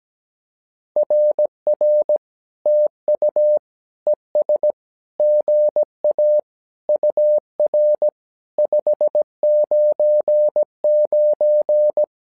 Morse code, heard as a keyed tone, about 17 wpm, 600 Hz. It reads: RR TU ES GA UR 599
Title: SOTA QSO Practice 17wpm